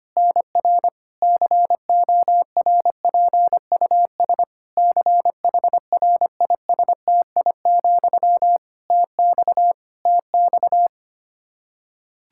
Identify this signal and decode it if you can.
Morse code, heard as a keyed tone, about 25 wpm, 700 Hz. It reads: NR CORPUS CHRISTI, TX TX
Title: K1USN SST QSO Practice 25wpm